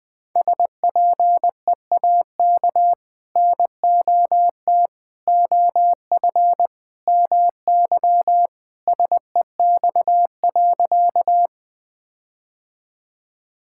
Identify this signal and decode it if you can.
Morse code, heard as a keyed tone, about 20 wpm, 700 Hz. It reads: SPEAK NOT OF MY SEX.
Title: Sentences from Top 1600 Words 20wpm